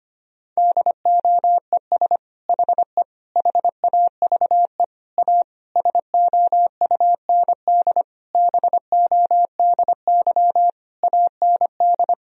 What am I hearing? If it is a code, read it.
Morse code, heard as a keyed tone, about 25 wpm, 700 Hz. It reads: DOES HE HAVE A SOUND BODY AND
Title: Sentences from Top 1000 Words - Review Part 2 - 25wpm